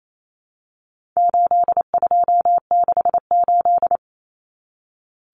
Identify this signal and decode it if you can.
Morse code, heard as a keyed tone, about 28 wpm, 700 Hz. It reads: 8268
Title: Numbers - 4-Digit 28wpm